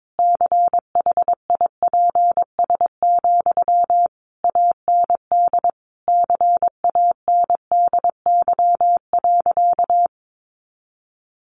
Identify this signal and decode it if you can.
Morse code, heard as a keyed tone, about 22 wpm, 700 Hz. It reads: CHIPS, AND CANDY.
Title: Easy Phrases 22wpm